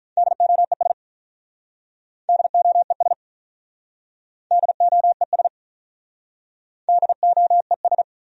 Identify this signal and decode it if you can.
Morse code, heard as a keyed tone, about 45 wpm, 700 Hz. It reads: DOES DOES DOES DOES